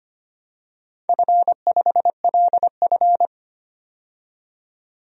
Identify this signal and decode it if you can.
Morse code, heard as a keyed tone, about 25 wpm, 700 Hz. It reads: F5LF
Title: DXCC-100 Call Signs - 4 Characters - 25wpm